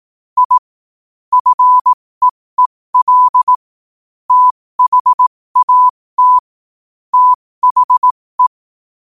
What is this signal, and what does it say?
Morse code, heard as a keyed tone, about 18 wpm, spaced as for 15 wpm, 1000 Hz. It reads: I FEEL THAT THE
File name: qotdfiles / 15wpm